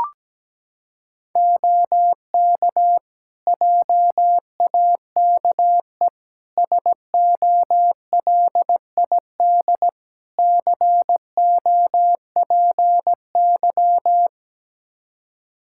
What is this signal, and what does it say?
Morse code, heard as a keyed tone, about 17 wpm, 700 Hz. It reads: OK JAKE SOLID COPY
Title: Practice QSO - Exchange 3 - More Info - 17wpm